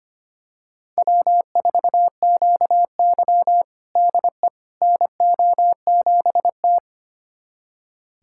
Morse code, heard as a keyed tone, about 25 wpm, 700 Hz. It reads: W4QY DE NO7T
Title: USA Callsign DE Callsign - 4 Characters - 25wpm